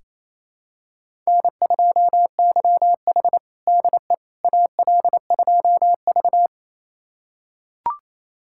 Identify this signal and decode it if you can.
Morse code, heard as a keyed tone, about 28 wpm, 700 Hz. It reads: N2YH DE AL2V
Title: USA Callsign DE Callsign - 4 Characters - 28wpm